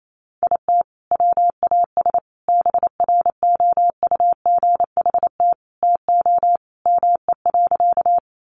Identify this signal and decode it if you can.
Morse code, heard as a keyed tone, about 28 wpm, 700 Hz. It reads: IT WAS BROUGHT TO ME.
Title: Sentences from Top 500 Words 28wpm